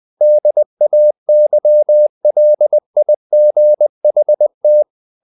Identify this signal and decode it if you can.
Morse code, heard as a keyed tone, about 20 wpm, 600 Hz. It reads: DAYLIGHT
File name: the old man and the sea / the old man and the sea part 1 20wpm